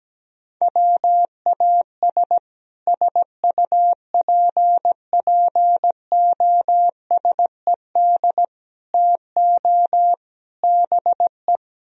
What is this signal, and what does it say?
Morse code, heard as a keyed tone, about 18 wpm, 700 Hz. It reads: WAS SUPPOSED TO BE
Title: Speed-Racing - Sets of 4 Words 17wpm